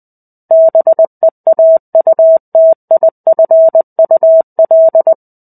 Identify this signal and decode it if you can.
Morse code, heard as a keyed tone, about 20 wpm, 650 Hz. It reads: BEAUTIFUL
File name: 05-Chapter-05 20 WPM